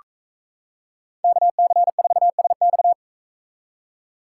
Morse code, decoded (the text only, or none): KC4SX